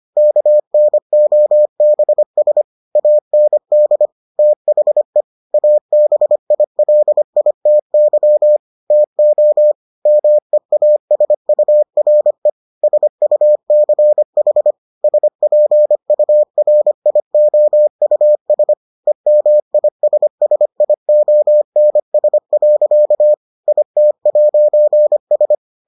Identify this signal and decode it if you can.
Morse code, heard as a keyed tone, about 25 wpm, 600 Hz. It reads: KNOBS AND THE ABILITY TO MEASURE SUCH SPURIOUS EMISSIONS. IT'S